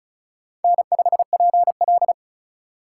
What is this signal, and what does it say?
Morse code, heard as a keyed tone, about 34 wpm, 700 Hz. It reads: N5PL
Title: USA Call Signs - 4 Characters - 3x Character Spacing - 35wpm